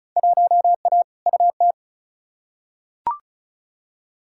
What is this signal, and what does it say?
Morse code, heard as a keyed tone, about 36 wpm, 700 Hz. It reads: 1A UT
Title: Field Day Exchanges - 35wpm